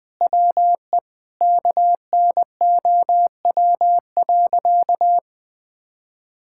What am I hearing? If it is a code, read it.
Morse code, heard as a keyed tone, about 20 wpm, 700 Hz. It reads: WE KNOW.
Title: Sentences with 8-Word Phrases - 20wpm